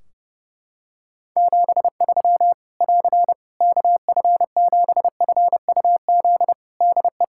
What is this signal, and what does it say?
Morse code, heard as a keyed tone, about 30 wpm, 700 Hz. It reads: 73 <AR> KF7FUZ DE
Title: Practice QSO - Exchange 4 - Ending - 30wpm